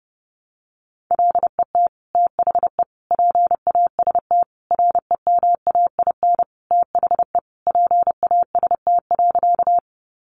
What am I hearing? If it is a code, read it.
Morse code, heard as a keyed tone, about 30 wpm, 700 Hz. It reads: LET THE PAST REMAIN THE PAST.
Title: Sentences from Top 1200 Words 30wpm